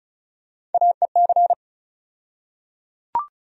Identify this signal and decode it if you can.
Morse code, heard as a keyed tone, about 36 wpm, 700 Hz. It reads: AEC
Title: USA Call Sign Postfixes - 3 Letters 35wpm